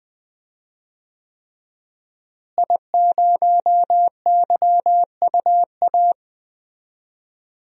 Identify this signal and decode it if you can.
Morse code, heard as a keyed tone, about 20 wpm, 700 Hz. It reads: I0YUA
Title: DXCC-100 Call Signs - 5 Characters - Repeated 3x - 20wpm